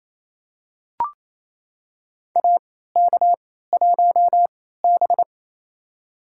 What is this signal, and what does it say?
Morse code, heard as a keyed tone, about 28 wpm, spaced as for 16 wpm, 700 Hz. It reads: AK1B
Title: USA Call Signs - 4 Characters - 3x Character Spacing - 28wpm